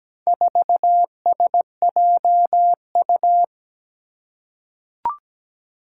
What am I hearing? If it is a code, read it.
Morse code, heard as a keyed tone, about 17 wpm, 700 Hz. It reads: 4SJU
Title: Four Challenging Letter-Number Combinations 17wpm